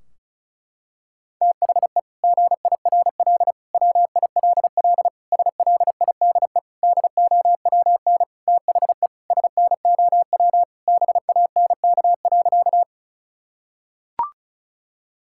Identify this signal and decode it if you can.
Morse code, heard as a keyed tone, about 36 wpm, 700 Hz. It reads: THE GIRL WILL SLIDE DOWN THE SNOW BANK.
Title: Sentences from Top 1600 Words 35wpm